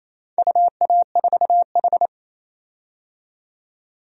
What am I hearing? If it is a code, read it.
Morse code, heard as a keyed tone, about 28 wpm, 700 Hz. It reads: UA4H
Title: DXCC-100 Call Signs - 4 Characters - 28wpm